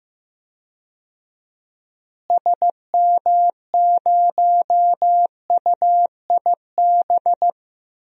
Morse code, heard as a keyed tone, about 15 wpm, 700 Hz. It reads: SM0UIB
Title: DXCC-100 Call Signs - 6 Characters - Repeated 3x - 15wpm